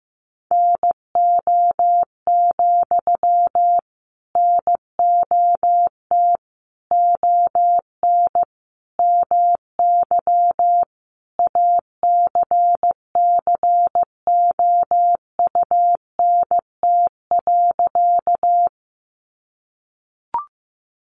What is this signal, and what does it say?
Morse code, heard as a keyed tone, about 15 wpm, 700 Hz. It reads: NO, NOT ON MY ACCOUNT.
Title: Sentences from Top 900 Words 15wpm